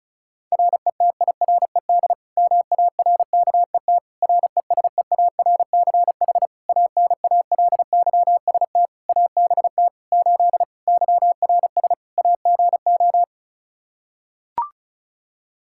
Morse code, heard as a keyed tone, about 36 wpm, 700 Hz. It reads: RETIRED MARKET RESEARCH ANALYST ABT 8 YRS AGO
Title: Practice QSO - Exchange 3 - More Info - 35wpm